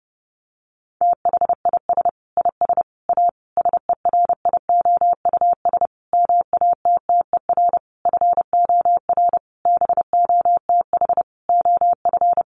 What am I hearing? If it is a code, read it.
Morse code, heard as a keyed tone, about 30 wpm, 700 Hz. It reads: THIS IS A SERIOUS MATTER FOR BOTH OF
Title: Sentences from Top 1100 Words 30wpm